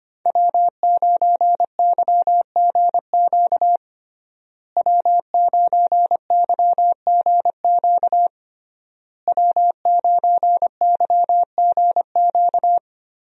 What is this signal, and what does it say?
Morse code, heard as a keyed tone, about 25 wpm, 700 Hz. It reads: W9YGQ W9YGQ W9YGQ
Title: USA Call Signs - 5 Characters - Repeated 3x - 25wpm